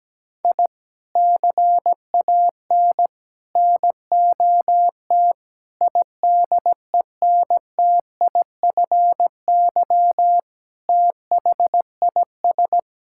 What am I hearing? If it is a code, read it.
Morse code, heard as a keyed tone, about 17 wpm, 700 Hz. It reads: I CAN NOT IDENTIFY THIS
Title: Sentences from Top 1000 Words - Review Part 6 - 17wpm